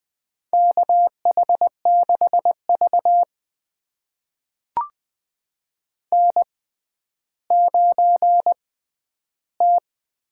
Morse code, heard as a keyed tone, about 20 wpm, spaced as for 11 wpm, 700 Hz. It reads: KH6V N9T
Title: USA Call Signs - 4 Characters - 6x Character Spacing - 20wpm